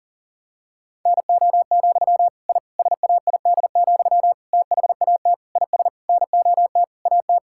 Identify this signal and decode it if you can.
Morse code, heard as a keyed tone, about 37 wpm, 700 Hz. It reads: NO, I SAID, THAT IS NOT AT
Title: Sentences from Top 1000 Words - Review Part 6 - 40wpm